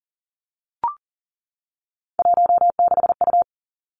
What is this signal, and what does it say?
Morse code, heard as a keyed tone, about 41 wpm, 700 Hz. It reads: 16U
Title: Three Challenging Letter-Number Combinations 40wpm